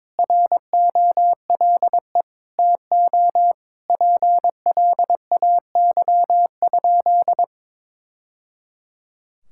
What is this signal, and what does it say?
Morse code, heard as a keyed tone, about 22 wpm, 700 Hz. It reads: ROLE TO PLAY?
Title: Sentences from Top 1000 Words - Review Part 6 - 22wpm